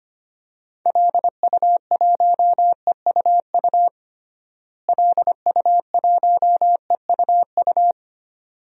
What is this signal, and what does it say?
Morse code, heard as a keyed tone, about 25 wpm, 700 Hz. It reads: LU1EUU LU1EUU
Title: DXCC-100 Call Signs - 6 Characters - Repeated 2x - 25wpm